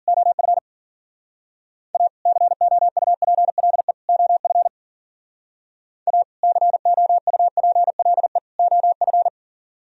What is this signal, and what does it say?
Morse code, heard as a keyed tone, about 49 wpm, 700 Hz. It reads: OF A COUPLE OF A COUPLE OF
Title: Speed-Racing - Sets of 3 Words 40wpm